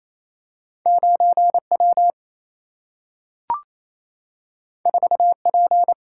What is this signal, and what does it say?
Morse code, heard as a keyed tone, about 28 wpm, 700 Hz. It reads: 9 W 4 P
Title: Single-Letter Single-Number 28wpm